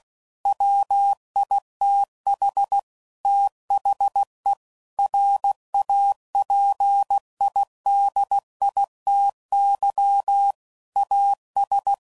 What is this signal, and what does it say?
Morse code, heard as a keyed tone, about 16 wpm, 800 Hz. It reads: WITH THE RAPIDITY AS
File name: CW Code Groups @15WPM / Technical Subjects / XL